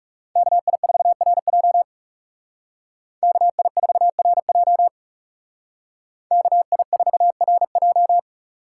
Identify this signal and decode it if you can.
Morse code, heard as a keyed tone, about 36 wpm, 700 Hz. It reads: KI4RJ KI4RJ KI4RJ
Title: Speed Racing - USA 2x2 Call Signs 30wpm